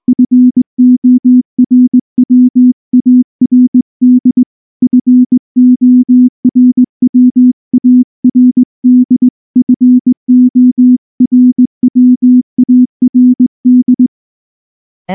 Morse code, heard as a keyed tone, about 20 wpm, 250 Hz. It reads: FORWARD FORWARD FORWARD
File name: morse-training-20WPM